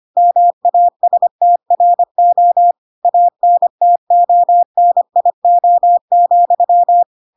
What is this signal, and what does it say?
Morse code, heard as a keyed tone, about 25 wpm, 700 Hz. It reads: MASTRO ANTONIO,
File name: Pinocchio / Pinocchio ch2 25 25 ik1hge